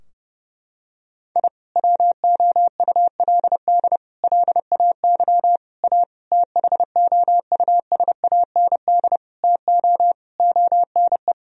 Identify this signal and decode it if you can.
Morse code, heard as a keyed tone, about 31 wpm, 700 Hz. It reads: I WOULD LAY A THOUSAND TO ONE
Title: Sentences from Top 1000 Words - Review Part 3 - 30wpm